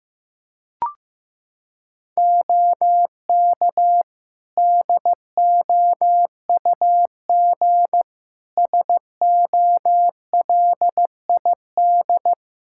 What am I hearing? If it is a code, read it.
Morse code, heard as a keyed tone, about 15 wpm, 700 Hz. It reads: OK DOUG SOLID
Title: K1USN SST QSO Practice 15wpm